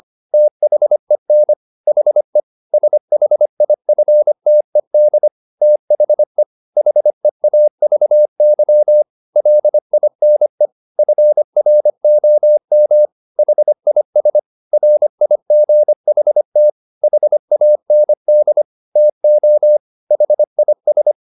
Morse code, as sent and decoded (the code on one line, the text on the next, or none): - .... -. / .... / ... .... .. ..-. - . -.. / - .... / .... .- ...- -.-- / .-.. .. -. / ..-. .-. --- -- / .... .. ... / .-. .. --. .... - / .... .- -. -.. / - --- / .... .. ...
THEN HE SHIFTED THE HEAVY LINE FROM HIS RIGHT HAND TO HIS